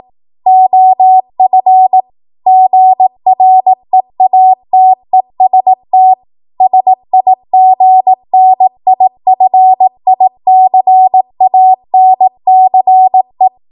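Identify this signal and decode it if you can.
Morse code, heard as a keyed tone, about 18 wpm, 750 Hz. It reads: OF GREATEST SIGNIFICANCE
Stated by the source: 18 WPM Morse code practice from a national amateur radio association's code archive.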